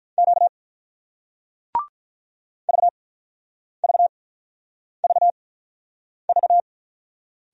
Morse code, heard as a keyed tone, about 48 wpm, spaced as for 9 wpm, 700 Hz. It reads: Q VVVV